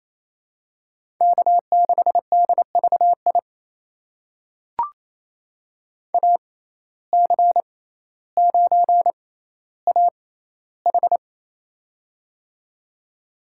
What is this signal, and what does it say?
Morse code, heard as a keyed tone, about 28 wpm, spaced as for 14 wpm, 700 Hz. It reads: K6DVI AC9AH
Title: USA Call Signs - 5 Characters - 6x Character Spacing - 28wpm